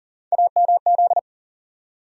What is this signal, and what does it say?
Morse code, heard as a keyed tone, about 40 wpm, 700 Hz. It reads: AMZ
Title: USA Call Sign Postfixes - 3 Letters 40wpm